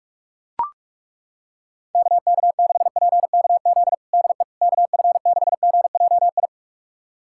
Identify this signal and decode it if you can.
Morse code, heard as a keyed tone, about 41 wpm, 700 Hz. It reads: KK6PKZ DE KF6GJI
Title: USA Callsign DE Callsign 45wpm